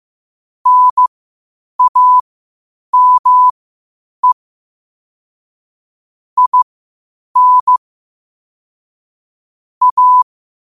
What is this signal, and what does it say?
Morse code, heard as a keyed tone, about 15 wpm, spaced as for 8 wpm, 1000 Hz. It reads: NAME IN A
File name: qotdfiles / 8wpm